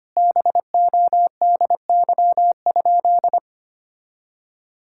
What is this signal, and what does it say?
Morse code, heard as a keyed tone, about 25 wpm, 700 Hz. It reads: BODY?
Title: Sentences from Top 1000 Words - Review Part 6 - 25wpm